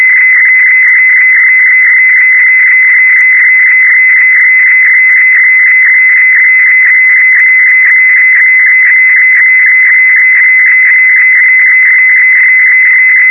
Weather fax (radiofax).
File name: europe / d-6068-edzw testbild dcf54-1994-rb